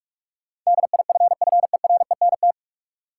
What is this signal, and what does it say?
Morse code, heard as a keyed tone, about 46 wpm, 700 Hz. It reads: DIFFERENT
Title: Sentences from Top 600 Words - New Word Drill - 45wpm